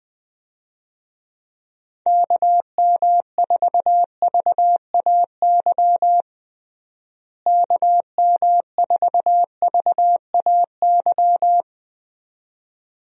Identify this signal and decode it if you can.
Morse code, heard as a keyed tone, about 19 wpm, 700 Hz. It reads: KM4VAY KM4VAY
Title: USA Call Signs - 6 Characters - Repeated 2x - 20wpm